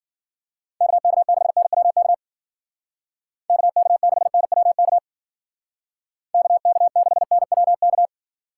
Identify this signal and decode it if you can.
Morse code, heard as a keyed tone, about 50 wpm, 700 Hz. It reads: KK6NWK KK6NWK KK6NWK
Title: Speed Racing - USA 2x3 Call Signs 40wpm